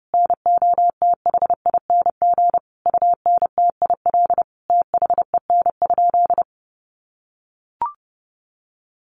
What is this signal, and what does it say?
Morse code, heard as a keyed tone, about 28 wpm, 700 Hz. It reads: NOTHING UNTIL THEN?
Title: Sentences from Top 500 Words 30wpm